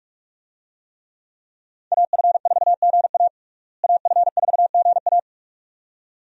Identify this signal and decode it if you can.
Morse code, heard as a keyed tone, about 43 wpm, 700 Hz. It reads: AF4GA AF4GA
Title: USA Call Signs - 5 Characters - Repeated 2x - 45wpm